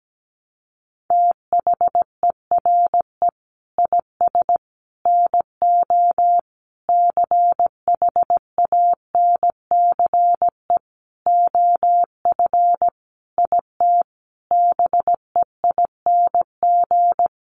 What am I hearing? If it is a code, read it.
Morse code, heard as a keyed tone, about 17 wpm, 700 Hz. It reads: THERE IS NO CHANCE OF IT BEING
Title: Sentences from Top 800 Words 17wpm